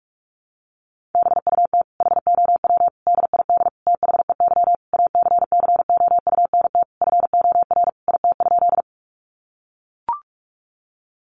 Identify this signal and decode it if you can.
Morse code, heard as a keyed tone, about 44 wpm, 700 Hz. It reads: BUT HOW DID THEY ACCOUNT FOR IT?
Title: Sentences from Top 900 Words 45wpm